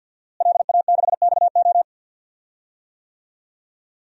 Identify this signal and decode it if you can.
Morse code, heard as a keyed tone, about 52 wpm, 700 Hz. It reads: LA6XO
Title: DXCC-100 Call Signs - 5 Characters - Repeated 3x - 50wpm